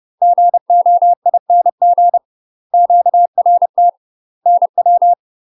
Morse code, heard as a keyed tone, about 30 wpm, 700 Hz. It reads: GOING QRT NW
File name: UK MASTER QSOs TRAINING / 30WPM / UK MASTER QSOs 0016